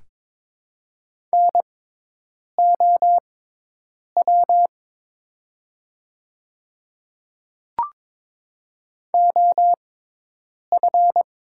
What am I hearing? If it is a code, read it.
Morse code, heard as a keyed tone, about 22 wpm, spaced as for 8 wpm, 700 Hz. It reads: NOW OF
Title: 6x Character Spacing - Top 100 Words - 22wpm